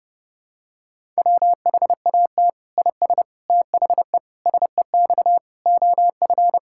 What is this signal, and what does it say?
Morse code, heard as a keyed tone, about 30 wpm, 700 Hz. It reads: WHAT IS THE SEX OF
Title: Sentences from Top 1600 Words 30wpm